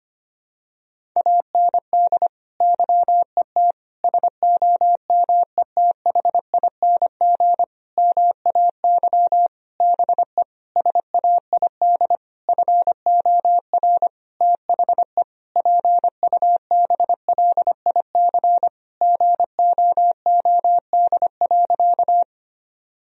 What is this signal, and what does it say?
Morse code, heard as a keyed tone, about 25 wpm, 700 Hz. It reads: AND YET SOMETHING MAY BE SAID FOR THE PUBLIC GOOD.
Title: Sentences from Top 1000 Words - Review Part 3 - 25wpm